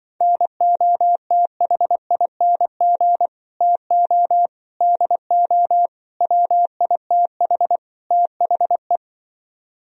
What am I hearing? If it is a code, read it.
Morse code, heard as a keyed tone, about 24 wpm, 700 Hz. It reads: NOTHING TO DO WITH THE
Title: Speed-Racing - Sets of 5 Words 20wpm